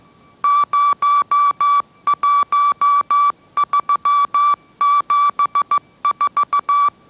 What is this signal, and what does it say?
Morse code, heard as a keyed tone, about 15 wpm, 1160 Hz. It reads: 01374
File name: mrx / mrx065 16wpm